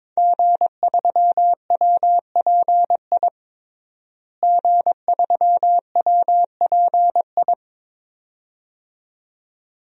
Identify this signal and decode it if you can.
Morse code, heard as a keyed tone, about 22 wpm, 700 Hz. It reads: G3WPI G3WPI
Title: DXCC-100 Call Signs - 5 Characters - Repeated 2x - 22wpm